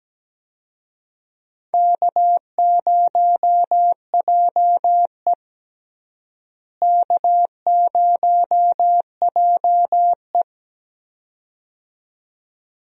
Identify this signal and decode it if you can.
Morse code, heard as a keyed tone, about 17 wpm, 700 Hz. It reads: K0JE K0JE
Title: USA Call Signs - 4 Characters - Repeated 2x - 17wpm